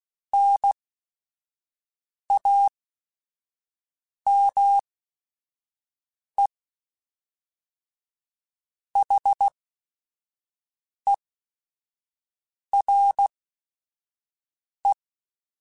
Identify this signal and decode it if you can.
Morse code, heard as a keyed tone, about 16 wpm, spaced as for 5 wpm, 800 Hz. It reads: NAME HERE
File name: QSOs / QSOs-mp3s / WD0BGP 1@5